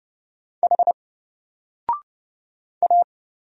Morse code, heard as a keyed tone, about 30 wpm, 700 Hz. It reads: H A
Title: Single Letter-Number Alt 30wpm